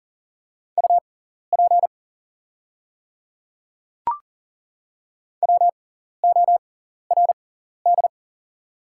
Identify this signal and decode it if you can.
Morse code, heard as a keyed tone, about 40 wpm, spaced as for 14 wpm, 700 Hz. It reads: UP WORD
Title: 6x Character Spacing - Top 100 Words - 40wpm